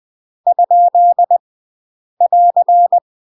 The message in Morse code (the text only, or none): ? <AR>